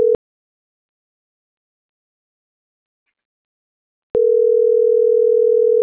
Ringback tone (North America), 440 + 480 Hz.